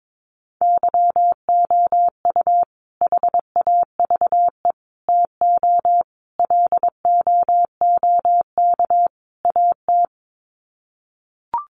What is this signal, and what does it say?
Morse code, heard as a keyed tone, about 22 wpm, 700 Hz. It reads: YOU HAVE TO LOOK AT
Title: Sets of 5 Words Encore 22wpm